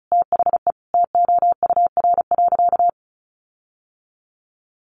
Morse code, heard as a keyed tone, about 34 wpm, 700 Hz. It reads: THE TOUR.
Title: Sentences from Top 1400 Words 35wpm